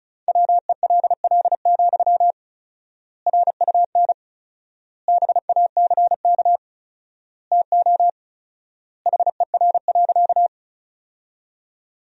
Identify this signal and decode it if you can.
Morse code, heard as a keyed tone, about 36 wpm, 700 Hz. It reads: WELL, RUN BACK TO HER.
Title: 4x Word-Spacing - Sentences from Top 300 Words - 35wpm